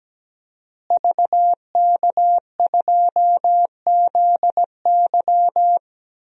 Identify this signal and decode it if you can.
Morse code, heard as a keyed tone, about 17 wpm, 700 Hz. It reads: VK2ZY
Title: DXCC-100 Call Signs - 5 Characters - Repeated 2x - 17wpm